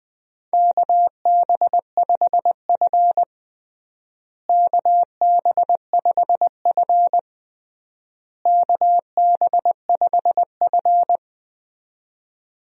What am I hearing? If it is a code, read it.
Morse code, heard as a keyed tone, about 20 wpm, 700 Hz. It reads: KB5F KB5F KB5F
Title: USA Call Signs - 4 Characters - Repeated 3x - 20wpm